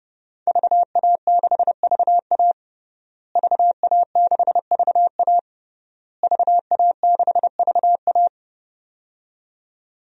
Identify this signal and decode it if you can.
Morse code, heard as a keyed tone, about 29 wpm, 700 Hz. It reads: VA6VA VA6VA VA6VA
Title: DXCC-100 Call Signs - 5 Characters - Repeated 3x - 30wpm